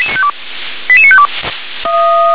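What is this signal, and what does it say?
Five-tone selective call (ZVEI), heard as quick runs of five tones, tones of about 70 ms. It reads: 88952
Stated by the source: ZVEI five-tone call received by SDR.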